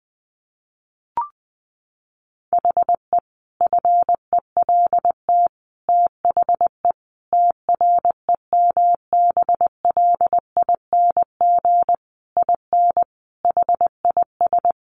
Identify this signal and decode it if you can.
Morse code, heard as a keyed tone, about 20 wpm, 700 Hz. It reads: HE FELT THE TREMBLING IN HIS
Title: A Sound Of Thunder 20wpm